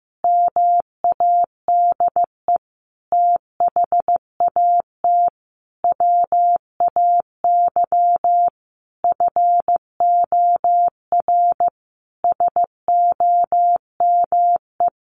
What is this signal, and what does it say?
Morse code, heard as a keyed tone, about 15 wpm, 700 Hz. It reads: MADE THAT WAY FOR SOME
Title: Sentences from Top 700 Words 15wpm